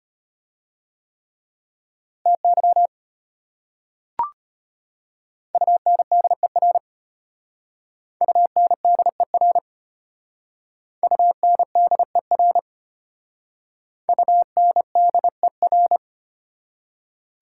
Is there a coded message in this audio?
Yes: Morse code from 2.3 to 16.0 s, heard as a keyed tone, about 33 wpm, 700 Hz. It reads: TY UNDER UNDER UNDER UNDER